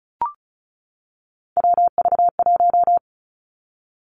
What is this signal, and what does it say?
Morse code, heard as a keyed tone, about 36 wpm, 700 Hz. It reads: WV1
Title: Three Challenging Letter-Number Combinations 35wpm